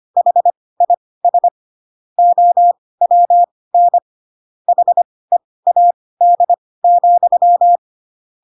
Morse code, heard as a keyed tone, about 25 wpm, spaced as for 18 wpm, 700 Hz. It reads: HIS OWN HEAD,
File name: Pinocchio ch3 12 25 ik1hge